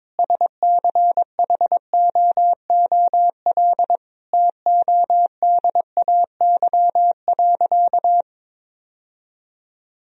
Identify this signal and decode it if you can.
Morse code, heard as a keyed tone, about 22 wpm, 700 Hz. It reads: SCHOOL TODAY.
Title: Sentences from Top 1000 Words - Review Part 1 - 22wpm